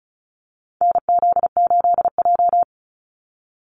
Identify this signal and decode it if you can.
Morse code, heard as a keyed tone, about 34 wpm, 700 Hz. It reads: NZ8J
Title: Call Signs Repeated 2x - 35wpm